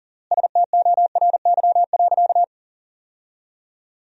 Morse code, heard as a keyed tone, about 40 wpm, 700 Hz. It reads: STORY.
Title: 3x Word-Spacing - Sentences from Top 300 Words - 40wpm